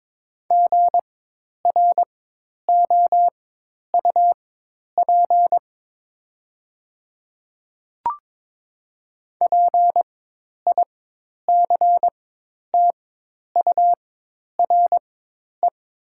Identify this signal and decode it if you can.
Morse code, heard as a keyed tone, about 22 wpm, spaced as for 10 wpm, 700 Hz. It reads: GROUP PICTURE
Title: 4x Character Spacing - Top 300 Words - 22wpm